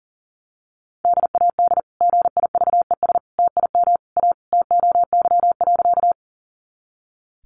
Morse code, heard as a keyed tone, about 37 wpm, 700 Hz. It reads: DAD GIVES TIM A TOY.
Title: Easy Phrases 40wpm